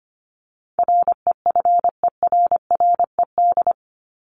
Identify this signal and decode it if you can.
Morse code, heard as a keyed tone, about 25 wpm, 700 Hz. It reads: REFERRED
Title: ED Suffix - Top 100 Words with Base Words 25wpm